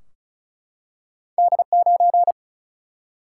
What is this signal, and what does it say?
Morse code, heard as a keyed tone, about 36 wpm, 700 Hz. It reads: D9
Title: Single-Letter Single-Number 35wpm